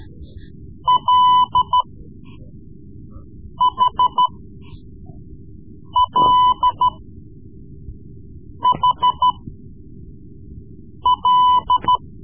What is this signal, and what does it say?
Morse code, heard as a keyed tone, about 13 wpm, spaced as for 4 wpm, 1000 Hz. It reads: LHLHL
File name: Morse Lessons / Morse Lessons / Tape-3B